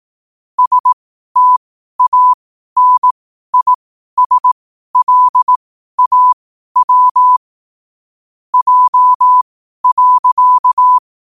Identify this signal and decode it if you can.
Morse code, heard as a keyed tone, about 18 wpm, spaced as for 12 wpm, 1000 Hz. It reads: STANISLAW J.
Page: New quotes every day in morse code at 12 Words per minute.